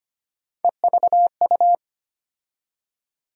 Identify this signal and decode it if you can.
Morse code, heard as a keyed tone, about 25 wpm, 700 Hz. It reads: EVU
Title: USA Call Sign Postfixes - 3 Letters 25wpm